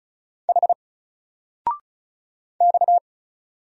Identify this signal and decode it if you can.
Morse code, heard as a keyed tone, about 34 wpm, 700 Hz. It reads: H X
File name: Single-Letter-Number-35wpm